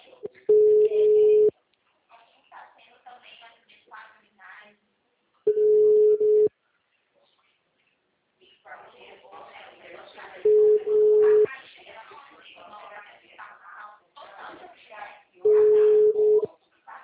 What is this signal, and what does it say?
Ringback tone (Europe, 425 Hz), with speech.